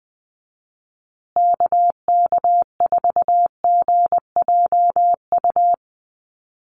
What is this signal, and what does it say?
Morse code, heard as a keyed tone, about 20 wpm, 700 Hz. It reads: KK4GJU
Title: USA Call Signs Repeated 3x 20wpm